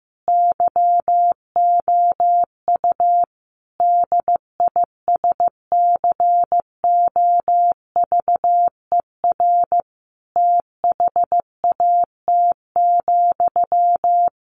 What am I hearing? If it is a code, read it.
Morse code, heard as a keyed tone, about 15 wpm, 700 Hz. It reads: YOU DISCOVER THAT,
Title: Sentences from Top 1100 Words 15wpm